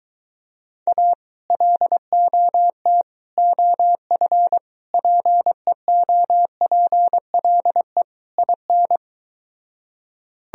Morse code, heard as a keyed tone, about 23 wpm, 700 Hz. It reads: A LOT OF PEOPLE IN